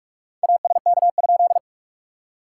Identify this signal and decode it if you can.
Morse code, heard as a keyed tone, about 44 wpm, 700 Hz. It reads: ASK?
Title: SOTA QSO Practice 45wpm